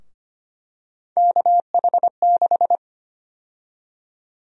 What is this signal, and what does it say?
Morse code, heard as a keyed tone, about 25 wpm, 700 Hz. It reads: KH6
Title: USA Call Sign Prefixes - 25wpm